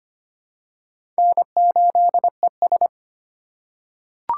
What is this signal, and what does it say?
Morse code, heard as a keyed tone, about 25 wpm, 700 Hz. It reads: N8ES
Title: USA Call Signs - 2x Character Spacing - 25wpm